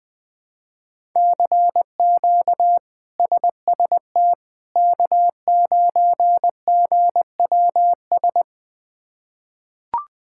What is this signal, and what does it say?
Morse code, heard as a keyed tone, about 20 wpm, 700 Hz. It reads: CQ SST K9GWS
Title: K1USN SST QSO Practice - 20wpm